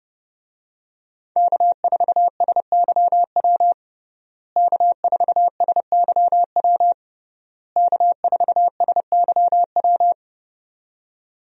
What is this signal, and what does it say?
Morse code, heard as a keyed tone, about 30 wpm, 700 Hz. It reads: K4SYW K4SYW K4SYW
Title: USA Call Signs - 5 Characters - Repeated 3x - 30wpm